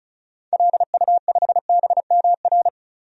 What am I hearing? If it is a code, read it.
Morse code, heard as a keyed tone, about 36 wpm, 700 Hz. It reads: LU5BMR
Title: DXCC-100 Call Signs - 6 Characters - 35wpm